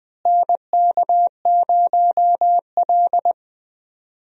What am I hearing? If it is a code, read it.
Morse code, heard as a keyed tone, about 21 wpm, 700 Hz. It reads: NK0L
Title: USA Call Signs - 4 Characters - Repeated 3x - 20wpm